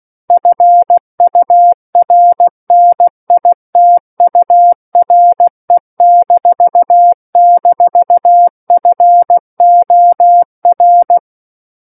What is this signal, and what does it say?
Morse code, heard as a keyed tone, about 16 wpm, 700 Hz. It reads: FURNITURE--FOR
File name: Book One / 07-Chapter 7 - 16 WPM